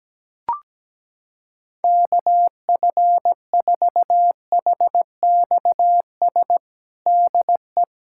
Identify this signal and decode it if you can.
Morse code, heard as a keyed tone, about 17 wpm, 700 Hz. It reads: KF4HXS DE
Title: USA Callsign DE Callsign - 6 Characters - 17wpm